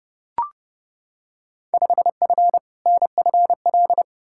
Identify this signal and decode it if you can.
Morse code, heard as a keyed tone, about 30 wpm, 700 Hz. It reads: 5F NFL
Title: Field Day Exchanges Encore - 30wpm